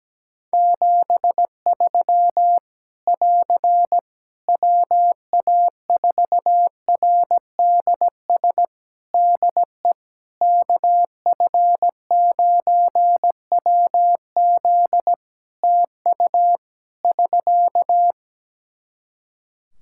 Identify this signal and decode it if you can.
Morse code, heard as a keyed tone, about 17 wpm, 700 Hz. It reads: 73 <AR> WA4RDS DE KF9WZ TU <SK>
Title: K1USN SST QSO Practice 17wpm